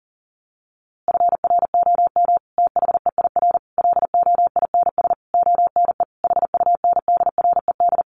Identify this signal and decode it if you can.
Morse code, heard as a keyed tone, about 40 wpm, 700 Hz. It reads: FROM THEIR LOINS ONE HUNDRED
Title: A Sound Of Thunder 40wpm